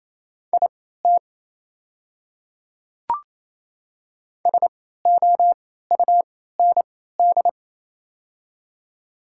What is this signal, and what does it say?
Morse code, heard as a keyed tone, about 28 wpm, spaced as for 16 wpm, 700 Hz. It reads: IT SOUND
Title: 3x Character Spacing - Top 200 Words - 28wpm